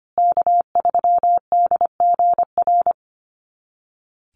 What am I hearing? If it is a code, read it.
Morse code, heard as a keyed tone, about 25 wpm, 700 Hz. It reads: K3DGR
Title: Call Signs Repeated 2x - 25wpm